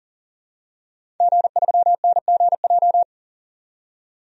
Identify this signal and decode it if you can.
Morse code, heard as a keyed tone, about 39 wpm, 700 Hz. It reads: G3NGJ
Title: DXCC-100 Call Signs - 5 Characters - 40wpm